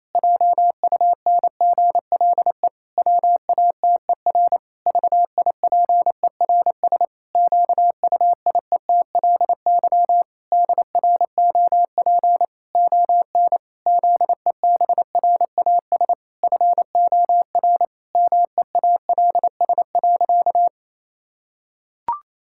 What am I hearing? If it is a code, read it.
Morse code, heard as a keyed tone, about 28 wpm, 700 Hz. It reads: JUNGLE WATER VIPERS QUIETLY DROP ON ZEBRAS FOR MEALS.
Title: Pangram Challenge v2 - 28wpm